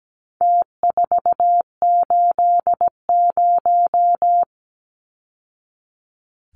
Morse code, heard as a keyed tone, about 17 wpm, 700 Hz. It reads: T480
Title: Numbers - Contest Style - 4-Digit 17wpm